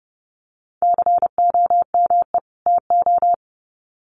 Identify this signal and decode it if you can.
Morse code, heard as a keyed tone, about 30 wpm, 700 Hz. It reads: COME TO
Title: Sets of 2 Words 30wpm